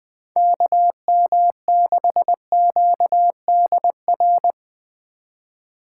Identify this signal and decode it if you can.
Morse code, heard as a keyed tone, about 21 wpm, 700 Hz. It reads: KM6QDR
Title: Speed Racing - USA 2x3 Call Signs 20wpm